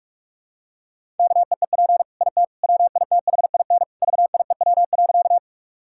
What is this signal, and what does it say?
Morse code, heard as a keyed tone, about 41 wpm, 700 Hz. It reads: KEEP IT WITHIN VIEW.
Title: Sentences from Top 1000 Words - Review Part 6 - 45wpm